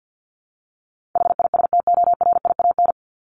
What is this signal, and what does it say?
Morse code, heard as a keyed tone, about 50 wpm, 700 Hz. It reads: HISTORIAN
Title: Top 1751-2000 Words 50wpm